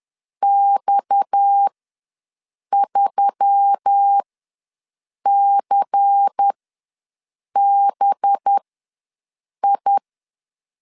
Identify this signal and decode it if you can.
Morse code, heard as a keyed tone, about 10 wpm, spaced as for 6 wpm, 800 Hz. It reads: X3CBI
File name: LEZIONI / 8 YJ45 / 8 YJ4 ws 10wpm 4020